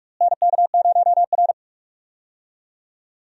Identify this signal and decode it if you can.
Morse code, heard as a keyed tone, about 47 wpm, 700 Hz. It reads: NK0R
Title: USA Call Signs - 4 Characters - Repeated 2x - 45wpm